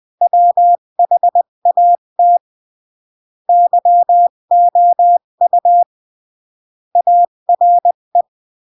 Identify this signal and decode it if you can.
Morse code, heard as a keyed tone, about 20 wpm, 700 Hz. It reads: WHAT YOU ARE
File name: CW / jodorowsky / 82 commandments 20-15